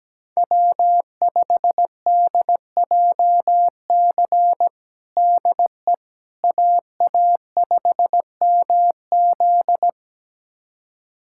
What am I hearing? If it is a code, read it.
Morse code, heard as a keyed tone, about 17 wpm, 700 Hz. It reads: W5DJC DE AA5MZ
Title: USA Callsign DE Callsign - 5 Characters - 17wpm